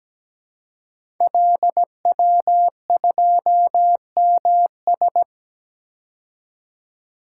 Morse code, heard as a keyed tone, about 17 wpm, 700 Hz. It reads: LW2MS
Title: Code Groups - 5 Letter:Numbers - 17wpm